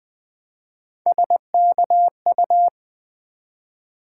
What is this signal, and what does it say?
Morse code, heard as a keyed tone, about 20 wpm, 700 Hz. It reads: SKU
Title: Build A Word - 1 Letter At A Time - 5-Letter Words - 20wpm